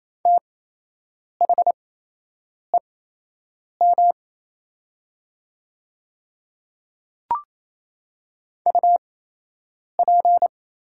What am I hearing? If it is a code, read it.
Morse code, heard as a keyed tone, about 28 wpm, spaced as for 8 wpm, 700 Hz. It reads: THEM UP
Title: 8x Character Spacing - Top 100 Words - 28wpm